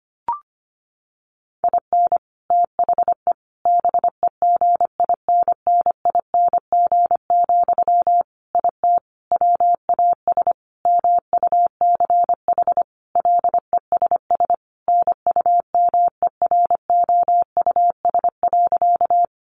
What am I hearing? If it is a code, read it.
Morse code, heard as a keyed tone, about 25 wpm, 700 Hz. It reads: IN THE BEGINNING, IT WAS MUCH LESS NUMEROUS.
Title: Sentences from Top 1400 Words 25wpm